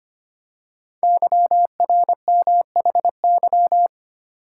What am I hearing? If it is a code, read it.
Morse code, heard as a keyed tone, about 25 wpm, 700 Hz. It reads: YRMHY
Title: Code Groups - 5 Letters - 25wpm